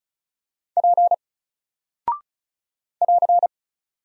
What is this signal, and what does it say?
Morse code, heard as a keyed tone, about 36 wpm, 700 Hz. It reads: P <AR>
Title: Single Character (Letter:Number:Punctuation:Prosign) 35wpm